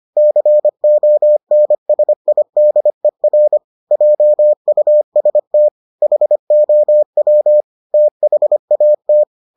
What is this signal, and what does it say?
Morse code, heard as a keyed tone, about 25 wpm, 600 Hz. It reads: CONSIDER JUST HOW THAT